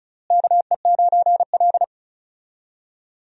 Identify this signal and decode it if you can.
Morse code, heard as a keyed tone, about 36 wpm, 700 Hz. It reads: KE9L
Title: USA Call Signs - 4 Characters - Repeated 3x - 35wpm